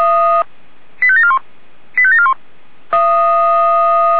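Five-tone selective call (ZVEI), heard as quick runs of five tones, tones of about 70 ms. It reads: 85741 85741
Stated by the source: ZVEI five-tone selective call received on an SDR.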